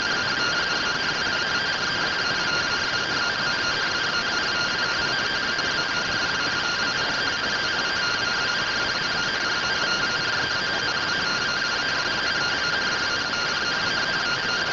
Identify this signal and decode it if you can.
Two-tone teleprinter signal, about 170 Hz shift.